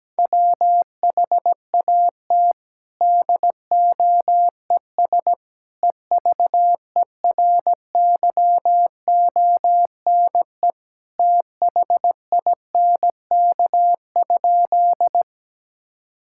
Morse code, heard as a keyed tone, about 17 wpm, 700 Hz. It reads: WHAT DOES EVERYONE THINK?
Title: Sentences from Top 1000 Words - Review Part 5 - 17wpm